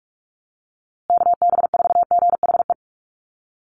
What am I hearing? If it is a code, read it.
Morse code, heard as a keyed tone, about 46 wpm, 700 Hz. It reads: KB4GHE
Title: USA Call Signs Repeated 3x 45wpm